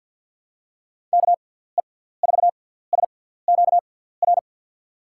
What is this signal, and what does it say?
Morse code, heard as a keyed tone, about 47 wpm, spaced as for 17 wpm, 700 Hz. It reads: KE4SQR
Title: USA Call Signs - 6 Characters - 6x Character Spacing - 50wpm